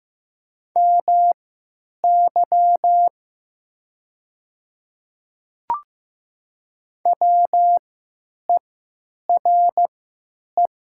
Morse code, heard as a keyed tone, about 15 wpm, spaced as for 8 wpm, 700 Hz. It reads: MY WERE
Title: 3x Character Spacing - Top 100 Words - 15wpm